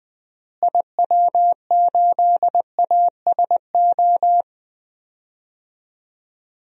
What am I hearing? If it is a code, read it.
Morse code, heard as a keyed tone, about 20 wpm, 700 Hz. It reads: IW8ASO
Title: DXCC-100 Call Signs - 6 Characters - Repeated 2x - 20wpm